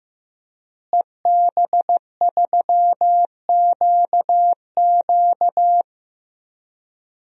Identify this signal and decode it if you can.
Morse code, heard as a keyed tone, about 16 wpm, 700 Hz. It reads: EB3QQ
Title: DXCC-100 Call Signs - 5 Characters - Repeated 2x - 15wpm